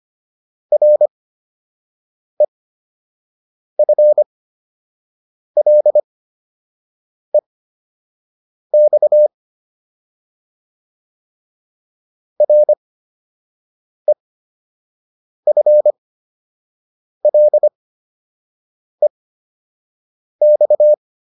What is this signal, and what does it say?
Morse code, heard as a keyed tone, about 25 wpm, spaced as for 6 wpm, 600 Hz. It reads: REFLEX REFLEX